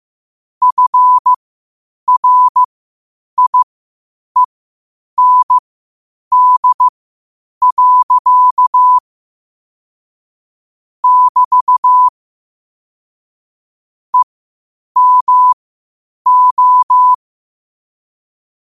Morse code, heard as a keyed tone, about 15 wpm, spaced as for 8 wpm, 1000 Hz. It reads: FRIEND. <BT> EMO
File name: qotdfiles / 8wpm